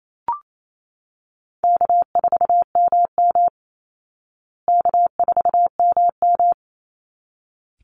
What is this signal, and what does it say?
Morse code, heard as a keyed tone, about 28 wpm, 700 Hz. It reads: K4MM K4MM
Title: Easy Call Signs Repeated 2x - 28wpm